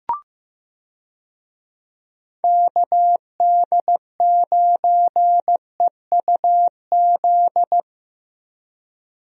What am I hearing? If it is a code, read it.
Morse code, heard as a keyed tone, about 15 wpm, 700 Hz. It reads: KD9EUZ
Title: USA Call Signs - 6 Characters - Repeated 3x - 15wpm